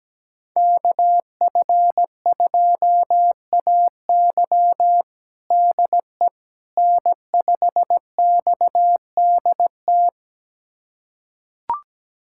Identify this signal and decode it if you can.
Morse code, heard as a keyed tone, about 17 wpm, 700 Hz. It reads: KF2AY DE N5XDT
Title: USA Callsign DE Callsign - 5 Characters - 17wpm